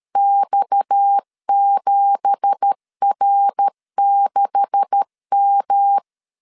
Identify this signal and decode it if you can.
Morse code, heard as a keyed tone, about 13 wpm, 800 Hz. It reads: X7R6M
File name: CW / DET / 1 9 1500C S A 12wpm 3800